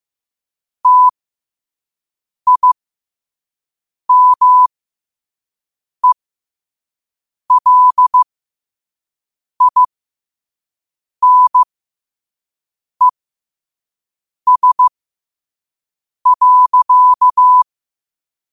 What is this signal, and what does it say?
Morse code, heard as a keyed tone, about 15 wpm, spaced as for 5 wpm, 1000 Hz. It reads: TIMELINES.
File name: qotdfiles / 5wpm